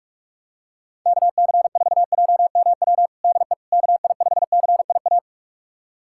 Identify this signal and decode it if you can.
Morse code, heard as a keyed tone, about 46 wpm, 700 Hz. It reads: KC4JMW DE KI5CIA
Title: USA Callsign DE Callsign 45wpm